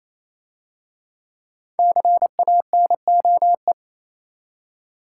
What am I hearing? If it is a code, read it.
Morse code, heard as a keyed tone, about 27 wpm, 690 Hz. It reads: CANOE
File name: Easy Words - Speed-Racing with Hints - 22wpm